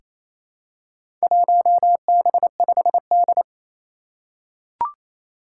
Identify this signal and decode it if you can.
Morse code, heard as a keyed tone, about 28 wpm, 700 Hz. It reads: 1B5D
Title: Four Challenging Letter-Number Combinations 28wpm